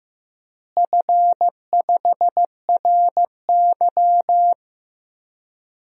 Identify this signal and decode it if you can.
Morse code, heard as a keyed tone, about 16 wpm, 700 Hz. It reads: F5RY
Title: DXCC-100 Call Signs - 4 Characters - Repeated 2x - 15wpm